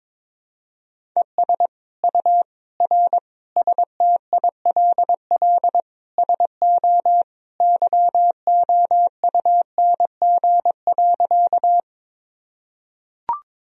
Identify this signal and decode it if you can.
Morse code, heard as a keyed tone, about 22 wpm, 700 Hz. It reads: ES U R STILL SO YOUNG.
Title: Sentences from Top 400 Words and Abbreviations 22wpm